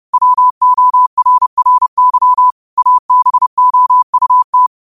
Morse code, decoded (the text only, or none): WORRY ABOUT